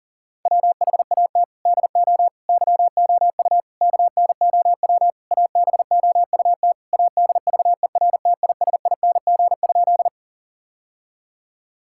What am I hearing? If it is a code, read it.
Morse code, heard as a keyed tone, about 40 wpm, 700 Hz. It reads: WHAT DO YOU KNOW ABOUT ADVERTISING?
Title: Sentences from Top 1000 Words - Review Part 6 - 40wpm